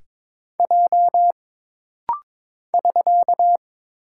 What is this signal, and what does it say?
Morse code, heard as a keyed tone, about 22 wpm, 700 Hz. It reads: J <SK>
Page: This rapid-fire timing is geared to push your ICR (Instant Character Recognition) skills.